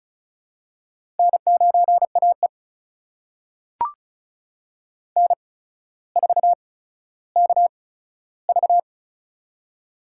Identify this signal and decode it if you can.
Morse code, heard as a keyed tone, about 36 wpm, spaced as for 15 wpm, 700 Hz. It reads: N9AE N4KV
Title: USA Call Signs - 4 Characters - 8x Character Spacing - 35wpm